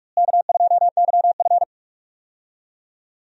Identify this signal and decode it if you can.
Morse code, heard as a keyed tone, about 43 wpm, 700 Hz. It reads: K2YF
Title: USA Call Signs - 4 Characters - Repeated 2x - 45wpm